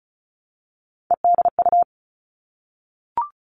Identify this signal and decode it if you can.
Morse code, heard as a keyed tone, about 36 wpm, 700 Hz. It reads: EDU
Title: Three Challenging Letter-Number Combinations 35wpm